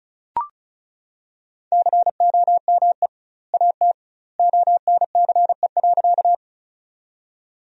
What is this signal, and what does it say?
Morse code, heard as a keyed tone, about 36 wpm, 700 Hz. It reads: COME AT ONCE.
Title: 2x Word-Spacing - Sentences from Top 300 Words - 35wpm